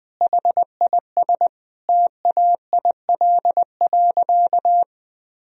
Morse code, heard as a keyed tone, about 20 wpm, 700 Hz. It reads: HIS TAIL.
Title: Sentences from Top 1000 Words - Review Part 3 - 20wpm